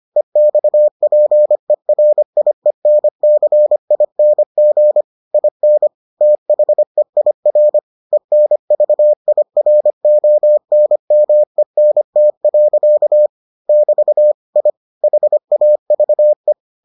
Morse code, heard as a keyed tone, about 25 wpm, 600 Hz. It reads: EXPERIENCING IN THEIR ENVIRONMENT. <BT> I HAVE